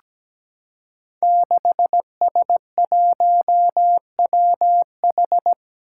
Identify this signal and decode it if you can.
Morse code, heard as a keyed tone, about 17 wpm, 700 Hz. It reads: 6S1WH
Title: Five Challenging Letter-Number Combinations 17wpm